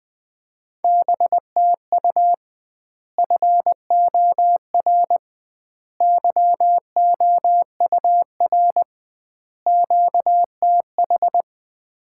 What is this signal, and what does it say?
Morse code, heard as a keyed tone, about 20 wpm, 700 Hz. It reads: BTU FOR YOUR QTH
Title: Common QSO Phrases - 2x Word-Spacing - 20wpm